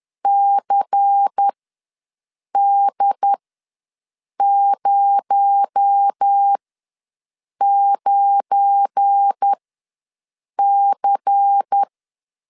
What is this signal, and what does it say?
Morse code, heard as a keyed tone, about 10 wpm, spaced as for 6 wpm, 800 Hz. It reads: CD09C
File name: LEZIONI / 8 YJ45 / 8 YJ ws 10wpm 4043